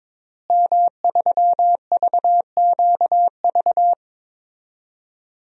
Morse code, heard as a keyed tone, about 22 wpm, 700 Hz. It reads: M3VQV
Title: DXCC-100 Call Signs - 5 Characters - Repeated 3x - 22wpm